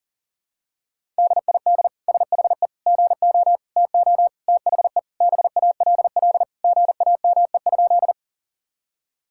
Morse code, heard as a keyed tone, about 40 wpm, 700 Hz. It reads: DID SHE GO TO THE BALL GAME?
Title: Sentences from Top 1000 Words - Review Part 6 - 40wpm